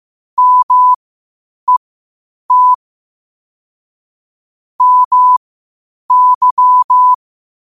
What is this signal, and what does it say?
Morse code, heard as a keyed tone, about 15 wpm, spaced as for 8 wpm, 1000 Hz. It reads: MET MY
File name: qotdfiles / 8wpm